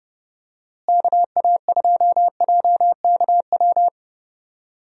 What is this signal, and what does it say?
Morse code, heard as a keyed tone, about 32 wpm, 700 Hz. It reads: KA2JKW
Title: USA Call Signs - 6 Characters - Repeated 3x - 30wpm